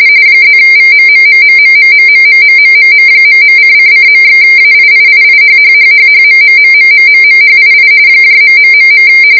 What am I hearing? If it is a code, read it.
RTTY, about 45.45 baud. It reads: +VMMMMMMMMMMMMM..:::./AIIIIHHHHHMMMMMMMHII ....